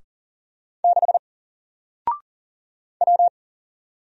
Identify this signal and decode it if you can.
Morse code, heard as a keyed tone, about 40 wpm, 700 Hz. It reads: 6 W
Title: Single Letter-Number 40wpm